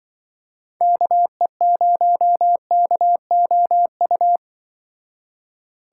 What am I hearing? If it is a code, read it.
Morse code, heard as a keyed tone, about 25 wpm, 700 Hz. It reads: KE0KOU
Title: Speed Racing - USA 2x3 Call Signs 20wpm